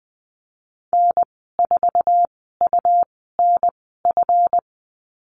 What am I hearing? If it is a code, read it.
Morse code, heard as a keyed tone, about 20 wpm, spaced as for 15 wpm, 700 Hz. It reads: N4UNF
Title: 186 Lesson 30g - Letter b - Call Signs 20wpm